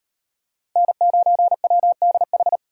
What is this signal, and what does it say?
Morse code, heard as a keyed tone, about 38 wpm, 700 Hz. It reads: N9WDH